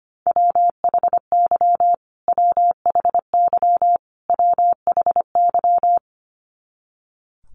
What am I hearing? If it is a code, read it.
Morse code, heard as a keyed tone, about 25 wpm, 700 Hz. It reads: WHY WHY WHY
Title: Top 200 Words Repeated 25wpm